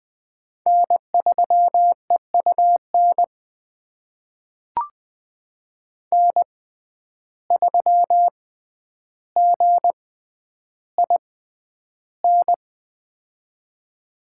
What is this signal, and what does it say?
Morse code, heard as a keyed tone, about 21 wpm, spaced as for 10 wpm, 700 Hz. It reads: N3EUN N3GIN
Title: USA Call Signs - 5 Characters - 6x Character Spacing - 20wpm